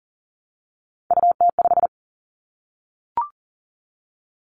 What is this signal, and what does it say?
Morse code, heard as a keyed tone, about 42 wpm, 700 Hz. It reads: UT5
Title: Callsign Prefixes for Top 25 DXCC - 40wpm